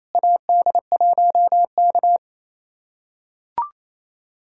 Morse code, heard as a keyed tone, about 28 wpm, 700 Hz. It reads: AD1K
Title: USA Callsign DE Callsign - 4 Characters - 28wpm